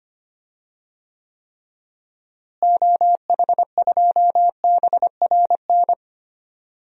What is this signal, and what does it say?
Morse code, heard as a keyed tone, about 25 wpm, 700 Hz. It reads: OH2BRN
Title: DXCC-100 Call Signs - 6 Characters - Repeated 3x - 25wpm